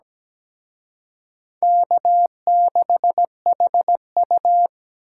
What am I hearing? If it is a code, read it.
Morse code, heard as a keyed tone, about 20 wpm, 700 Hz. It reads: K6HU
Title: Speed Racing - USA 1x2 Call Signs 17wpm